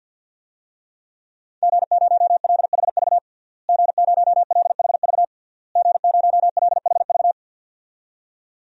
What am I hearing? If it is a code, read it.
Morse code, heard as a keyed tone, about 52 wpm, 700 Hz. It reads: G0LHV G0LHV G0LHV
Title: DXCC-100 Call Signs - 5 Characters - Repeated 3x - 50wpm